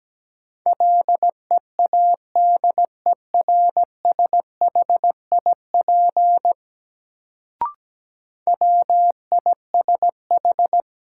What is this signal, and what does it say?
Morse code, heard as a keyed tone, about 17 wpm, 700 Hz. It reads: LEADERSHIP WISH
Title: Sentences from Top 1600 Words - New Word Drill Alt - 17wpm